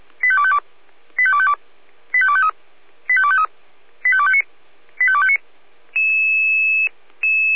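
Five-tone selective call (ZVEI), heard as quick runs of five tones, tones of about 70 ms. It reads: 85252 85252 85253 85253 85259 85259
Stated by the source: Five-tone ZVEI selective call 85218 received by SDR.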